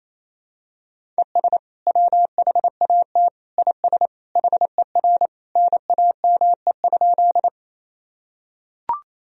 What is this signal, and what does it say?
Morse code, heard as a keyed tone, about 28 wpm, 700 Hz. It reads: ES WHAT IS HER NAME?
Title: Sentences from Top 200 Words and Abbreviations 28wpm